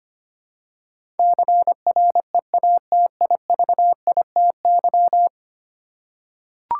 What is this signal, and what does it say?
Morse code, heard as a keyed tone, about 25 wpm, 700 Hz. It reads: CREATIVITY
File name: 10 Letter Words - First 3 Letters Given - 25wpm